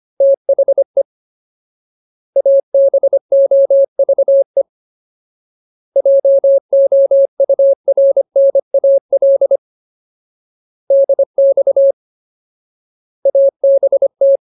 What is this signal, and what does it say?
Morse code, heard as a keyed tone, about 25 wpm, 550 Hz. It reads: THE ABOVE JOURNAL DX ABT
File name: Ultimate Word List / 25WPM Ultimate